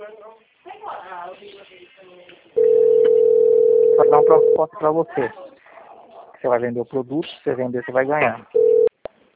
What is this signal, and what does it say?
Ringback tone (North America), 440 + 480 Hz, with speech.
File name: out- - - - - .7615